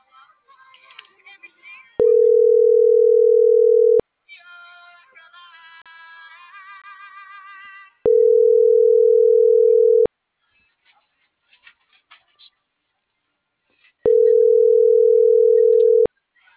Ringback tone (North America), 440 + 480 Hz.